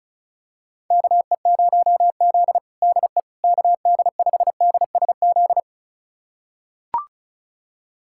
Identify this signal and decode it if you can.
Morse code, heard as a keyed tone, about 36 wpm, 700 Hz. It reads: KE0Z DE KD5DSZ
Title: USA Callsign DE Callsign 35wpm